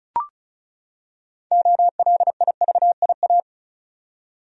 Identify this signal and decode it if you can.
Morse code, heard as a keyed tone, about 36 wpm, 700 Hz. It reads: OLIVIA
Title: USA Common Names - 35wpm